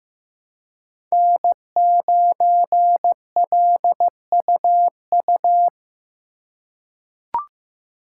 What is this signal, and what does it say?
Morse code, heard as a keyed tone, about 15 wpm, 700 Hz. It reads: N9LUU
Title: USA Call Signs - 5 Characters - 4x Character Spacing - 15wpm